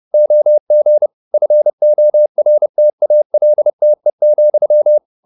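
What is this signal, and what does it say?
Morse code, heard as a keyed tone, about 30 wpm, 600 Hz. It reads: OG FORTALTE,
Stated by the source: High-speed Morse (QRQ) at 30 wpm.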